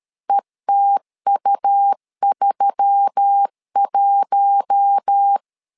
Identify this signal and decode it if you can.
Morse code, heard as a keyed tone, about 13 wpm, 800 Hz. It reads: ETU31
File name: LEZIONI / 5 LBVC / 5 LBV 12wpm 1855